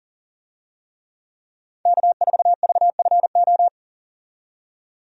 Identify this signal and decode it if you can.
Morse code, heard as a keyed tone, about 39 wpm, 700 Hz. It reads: K4VFO
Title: USA Call Signs - 5 Characters - 40wpm